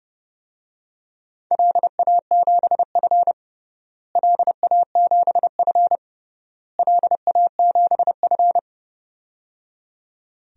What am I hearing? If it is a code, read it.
Morse code, heard as a keyed tone, about 31 wpm, 700 Hz. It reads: LA7F LA7F LA7F
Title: DXCC-100 Call Signs - 4 Characters - Repeated 3x - 30wpm